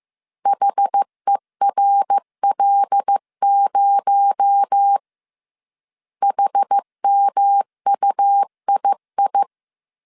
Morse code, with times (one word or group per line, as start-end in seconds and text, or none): 0.5-5.0 HERL0
6.2-9.4 HMUII